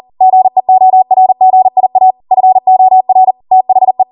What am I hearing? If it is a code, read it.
Morse code, heard as a keyed tone, about 40 wpm, 750 Hz. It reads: GEORGIA FOR THE